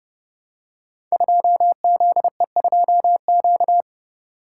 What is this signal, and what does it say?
Morse code, heard as a keyed tone, about 30 wpm, 700 Hz. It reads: 2ZE2Q
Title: Code Groups - 5 Letter:Numbers - 30wpm